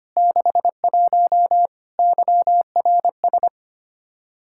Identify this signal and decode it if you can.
Morse code, heard as a keyed tone, about 25 wpm, 700 Hz. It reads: 61 YRS
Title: Practice QSO - Exchange 3 - More Info - 25wpm